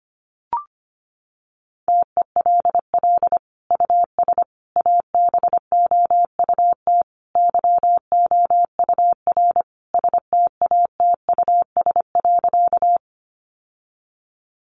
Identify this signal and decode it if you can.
Morse code, heard as a keyed tone, about 25 wpm, 700 Hz. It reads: TELL US ABOUT YOUR STATUS.
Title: Sentences from Top 1400 Words 25wpm